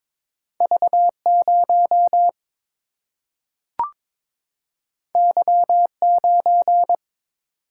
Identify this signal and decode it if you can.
Morse code, heard as a keyed tone, about 22 wpm, 700 Hz. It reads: V 0 Y 9
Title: Single-Letter Single-Number 22wpm